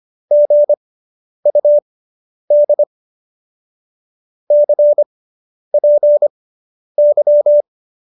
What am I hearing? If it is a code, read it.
Morse code, heard as a keyed tone, about 25 wpm, spaced as for 10 wpm, 590 Hz. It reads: GUD CPY